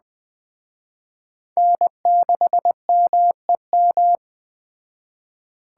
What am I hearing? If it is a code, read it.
Morse code, heard as a keyed tone, about 20 wpm, 700 Hz. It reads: N6MEM
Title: Speed Racing - USA 1x3 Call Signs 17wpm